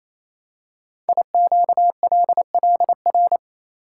Morse code, heard as a keyed tone, about 28 wpm, 700 Hz. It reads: IQLLR
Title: Code Groups - 5 Letters - 28wpm